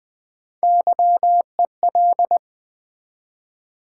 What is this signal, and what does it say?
Morse code, heard as a keyed tone, about 20 wpm, 700 Hz. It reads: YEL
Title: Build A Word - 3 Letters At A Time - 6-Letter Words - 20wpm